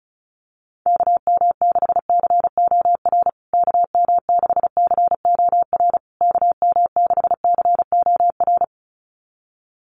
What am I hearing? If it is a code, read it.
Morse code, heard as a keyed tone, about 34 wpm, 700 Hz. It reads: KM6COR KM6COR KM6COR
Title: USA Call Signs Repeated 3x 35wpm